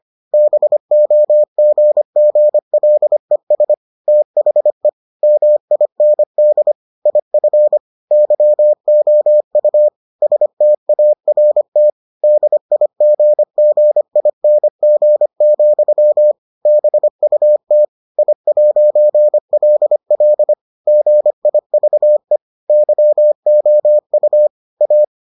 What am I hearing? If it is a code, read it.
Morse code, heard as a keyed tone, about 25 wpm, 600 Hz. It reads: BOGGLES THE MIND IF YOU START DIGGING, BUT I'LL GIVE YOU A